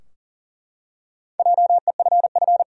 Morse code, heard as a keyed tone, about 40 wpm, 700 Hz. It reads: JEFF
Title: USA Common Names - Max 4 Letters - 40wpm